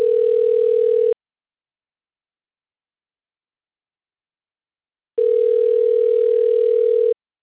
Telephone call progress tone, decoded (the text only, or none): ringback tone